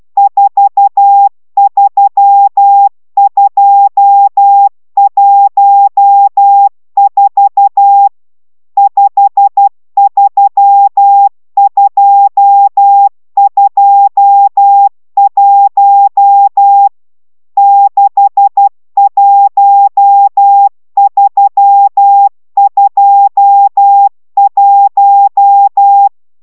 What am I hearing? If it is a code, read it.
Morse code, heard as a keyed tone, about 12 wpm, 800 Hz. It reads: 43214 53221 61321
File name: taviro tanulas / cw 69 22 10